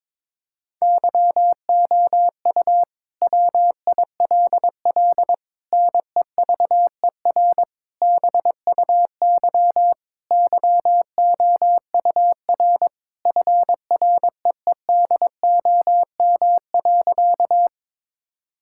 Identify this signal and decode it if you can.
Morse code, heard as a keyed tone, about 22 wpm, 700 Hz. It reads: YOU WILL NEVER BUY YOUR FREEDOM.
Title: Sentences from Top 1000 Words - Review Part 5 - 22wpm